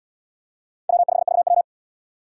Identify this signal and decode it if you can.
Morse code, heard as a keyed tone, about 73 wpm, 700 Hz. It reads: W5VW